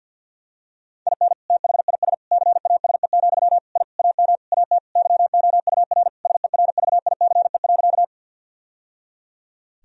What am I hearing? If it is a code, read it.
Morse code, heard as a keyed tone, about 50 wpm, 700 Hz. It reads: IN THIS CASE, I AM AT YOUR SERVICE.
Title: Sentences from Top 1000 Words - Review Part 4 - 50wpm